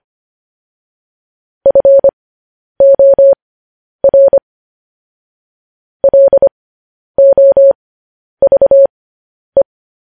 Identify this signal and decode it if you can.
Morse code, heard as a keyed tone, about 25 wpm, spaced as for 10 wpm, 550 Hz. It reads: FOR LOVE
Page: news headlines in morse code at 10 wpm